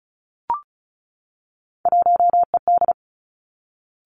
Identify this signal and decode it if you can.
Morse code, heard as a keyed tone, about 36 wpm, 700 Hz. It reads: 1ED
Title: Three Challenging Letter-Number Combinations 35wpm